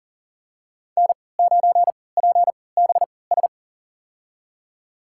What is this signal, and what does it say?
Morse code, heard as a keyed tone, about 40 wpm, spaced as for 23 wpm, 700 Hz. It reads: N9PBS
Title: USA Call Signs - 3x Character Spacing - 40wpm